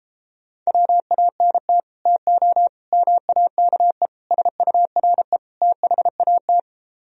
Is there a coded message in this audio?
Yes: Morse code from 0.7 to 6.6 s, heard as a keyed tone, about 33 wpm, 700 Hz. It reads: WANT TO MAKE SURE THAT